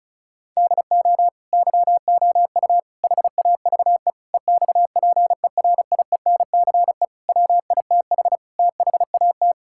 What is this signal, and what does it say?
Morse code, heard as a keyed tone, about 36 wpm, 700 Hz. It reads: DO YOU HAVE EXPERIENCE WITH THAT
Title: Sentences from Top 1000 Words - Review Part 5 - 35wpm